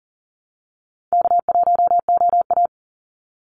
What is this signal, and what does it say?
Morse code, heard as a keyed tone, about 40 wpm, 700 Hz. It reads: K1OA
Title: Call Signs Repeated 3x - 40wpm